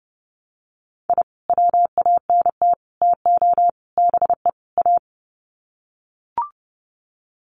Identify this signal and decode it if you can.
Morse code, heard as a keyed tone, about 30 wpm, 700 Hz. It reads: I WANT TO BE A
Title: Sets of 5 Words Encore 30wpm